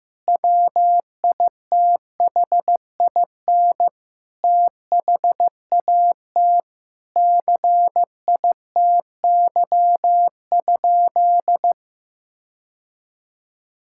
Morse code, heard as a keyed tone, about 16 wpm, 700 Hz. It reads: WITHIN THAT CITY?
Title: Sentences from Top 1000 Words - Review Part 5 - 15wpm